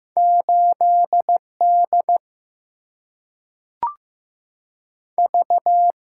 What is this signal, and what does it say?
Morse code, heard as a keyed tone, about 15 wpm, 700 Hz. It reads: 8 D V
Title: Single-Letter Single-Number 15wpm